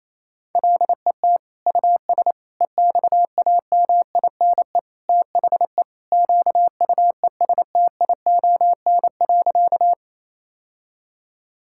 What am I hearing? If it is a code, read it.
Morse code, heard as a keyed tone, about 28 wpm, 700 Hz. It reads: LET US EXAMINE THE QUESTION.
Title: Sentences from Top 1600 Words 28wpm